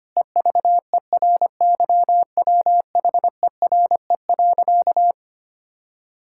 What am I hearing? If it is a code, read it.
Morse code, heard as a keyed tone, about 25 wpm, 700 Hz. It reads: EVERYWHERE.
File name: practice / 25wpm / Difficult Sentences with Hints - 25wpm